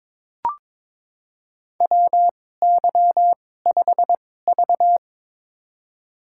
Morse code, heard as a keyed tone, about 21 wpm, spaced as for 16 wpm, 700 Hz. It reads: WY5V
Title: USA Call Signs - 4 Characters - 2x Character Spacing - 22wpm